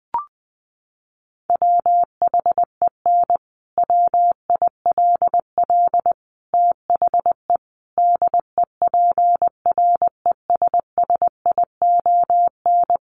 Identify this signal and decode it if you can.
Morse code, heard as a keyed tone, about 20 wpm, 700 Hz. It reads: WHEN WILL THE DEPRESSION
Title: Sentences from Top 1300 Words 20wpm